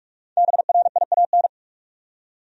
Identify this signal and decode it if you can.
Morse code, heard as a keyed tone, about 46 wpm, 700 Hz. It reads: BRIAN
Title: USA Common Names - 45wpm